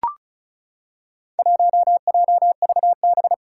Morse code, heard as a keyed tone, about 34 wpm, 700 Hz. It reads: 1JVB
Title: Four Challenging Letter-Number Combinations 35wpm